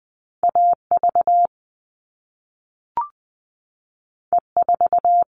Morse code, heard as a keyed tone, about 20 wpm, 700 Hz. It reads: AV E4
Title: All Cut Numbers - 2-Digit 20wpm